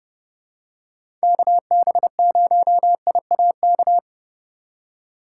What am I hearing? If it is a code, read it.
Morse code, heard as a keyed tone, about 32 wpm, 700 Hz. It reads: KB0IAK
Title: USA Call Signs - 6 Characters - 4x Character Spacing - 30wpm